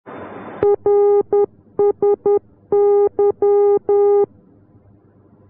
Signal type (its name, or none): Morse code